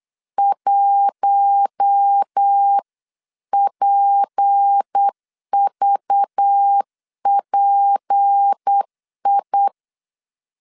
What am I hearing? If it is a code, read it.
Morse code, heard as a keyed tone, about 8 wpm, 800 Hz. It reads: 1PVPI
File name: LEZIONI / 6 PKX0 / 6 P 8wpm 3025